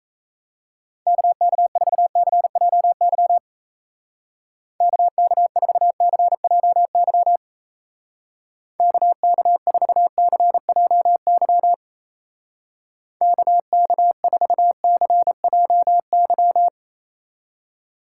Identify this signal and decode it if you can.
Morse code, heard as a keyed tone, about 35 wpm, 700 Hz. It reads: KK4CJY KK4CJY KK4CJY KK4CJY
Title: Speed Racing - USA Call Signs 28wpm